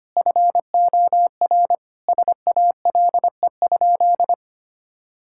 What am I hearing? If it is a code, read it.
Morse code, heard as a keyed tone, about 25 wpm, 700 Hz. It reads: FOR SALE?
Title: Sentences from Top 1600 Words 25wpm